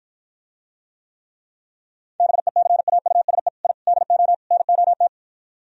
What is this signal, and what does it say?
Morse code, heard as a keyed tone, about 52 wpm, 700 Hz. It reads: BECAUSE I DO NOT